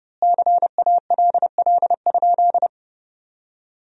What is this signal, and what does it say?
Morse code, heard as a keyed tone, about 30 wpm, 700 Hz. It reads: CALL?
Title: 3x Word-Spacing - Sentences from Top 100 Words - 30wpm